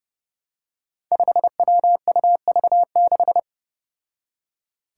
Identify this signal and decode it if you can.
Morse code, heard as a keyed tone, about 29 wpm, 700 Hz. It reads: 5WUV6
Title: Five Challenging Letter-Number Combinations 30wpm